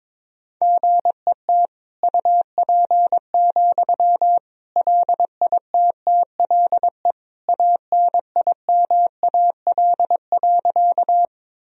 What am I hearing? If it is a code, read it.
Morse code, heard as a keyed tone, about 22 wpm, 700 Hz. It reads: GET UP, LITTLE ANIMAL.
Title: Sentences from Top 1000 Words - Review Part 1 - 22wpm